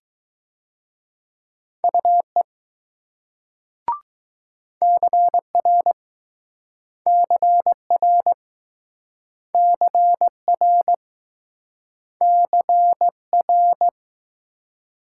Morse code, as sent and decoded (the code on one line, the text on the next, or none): ..- . / -.-. .-. / -.-. .-. / -.-. .-. / -.-. .-.
UE CR CR CR CR